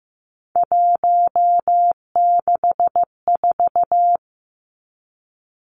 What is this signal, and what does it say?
Morse code, heard as a keyed tone, about 15 wpm, 700 Hz. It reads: 164
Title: Numbers - 3-Digit 15wpm